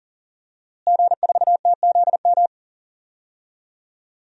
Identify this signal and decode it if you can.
Morse code, heard as a keyed tone, about 40 wpm, 700 Hz. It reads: G4TZM
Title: DXCC-100 Call Signs - 5 Characters - Repeated 2x - 40wpm